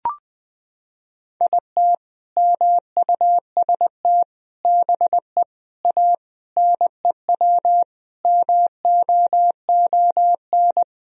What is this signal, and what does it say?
Morse code, heard as a keyed tone, about 20 wpm, 700 Hz. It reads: IT MUST BE A NEW MOON
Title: Sentences from Top 500 Words and Abbreviations 20wpm